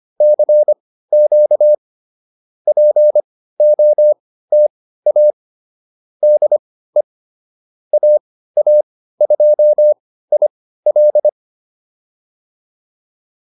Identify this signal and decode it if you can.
Morse code, heard as a keyed tone, about 25 wpm, spaced as for 15 wpm, 600 Hz. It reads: CQ POTA DE AA2IL
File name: POTA104 15